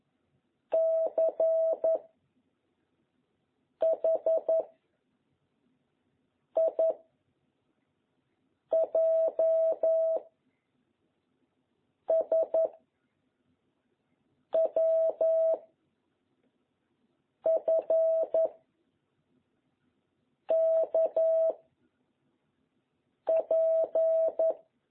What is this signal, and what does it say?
Morse code, heard as a keyed tone, about 11 wpm, 660 Hz. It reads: C H I J S W F K P
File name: jh8cbh / 2021cw